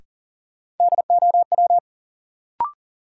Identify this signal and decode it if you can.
Morse code, heard as a keyed tone, about 40 wpm, 700 Hz. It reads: DOW
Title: Sets of 5 Words - New Word Drill - 40wpm